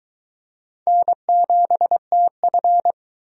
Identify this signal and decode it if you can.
Morse code, heard as a keyed tone, about 22 wpm, 700 Hz. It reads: N7TF